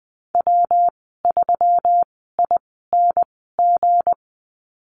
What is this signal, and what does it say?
Morse code, heard as a keyed tone, about 19 wpm, spaced as for 14 wpm, 700 Hz. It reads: W3ING
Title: 193 Lesson 31g - Letter v - Call Signs 20wpm